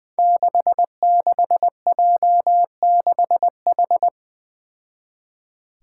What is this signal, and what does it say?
Morse code, heard as a keyed tone, about 20 wpm, 700 Hz. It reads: 66J6H
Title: Five Challenging Letter-Number Combinations 20wpm